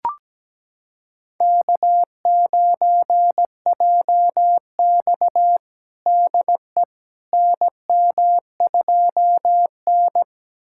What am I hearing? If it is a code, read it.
Morse code, heard as a keyed tone, about 17 wpm, 700 Hz. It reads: K9JX DE NM2N
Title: USA Callsign DE Callsign - 4 Characters - 17wpm